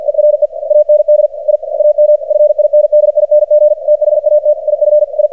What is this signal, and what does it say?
Morse code, heard as a keyed tone, about 51 wpm, 600 Hz, with radio noise. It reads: ERE AND I AM FINDING THAT IF I
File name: archived / CRSnet